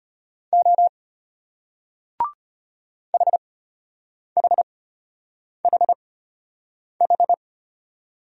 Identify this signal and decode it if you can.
Morse code, heard as a keyed tone, about 34 wpm, spaced as for 8 wpm, 700 Hz. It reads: O HHHH